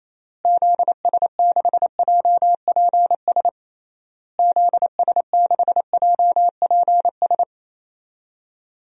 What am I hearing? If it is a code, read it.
Morse code, heard as a keyed tone, about 27 wpm, 700 Hz. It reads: ZS6JPS ZS6JPS
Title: DXCC-100 Call Signs - 6 Characters - Repeated 3x - 28wpm